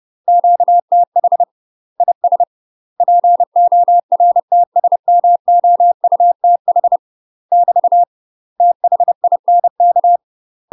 Morse code, heard as a keyed tone, about 30 wpm, 700 Hz. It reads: QTH IS PORTSMOUTH <BT> THINK
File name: UK MASTER QSOs TRAINING / 30WPM / UK MASTER QSOs 0049